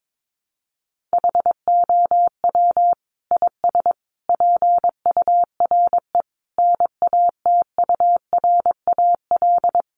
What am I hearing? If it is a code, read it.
Morse code, heard as a keyed tone, about 22 wpm, 700 Hz. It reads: HOW IS PURE NATURAL
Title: Sentences from Top 1500 Words 22wpm